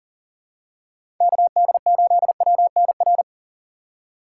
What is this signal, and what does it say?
Morse code, heard as a keyed tone, about 40 wpm, 700 Hz. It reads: KD8WNR
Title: USA Call Signs - 6 Characters - 3x Character Spacing - 40wpm